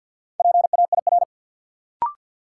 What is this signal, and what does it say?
Morse code, heard as a keyed tone, about 53 wpm, 700 Hz. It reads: PAIR
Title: Sets of 3 Words - New Word Drill - 50wpm